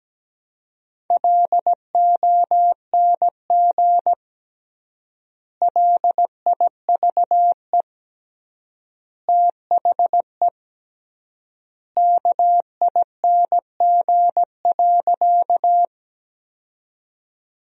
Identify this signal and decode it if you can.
Morse code, heard as a keyed tone, about 17 wpm, 700 Hz. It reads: LONG LIVE THE KING.
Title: 3x Word-Spacing - Sentences from Top 400 Words - 17wpm